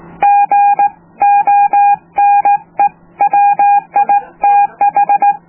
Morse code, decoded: GONE WITH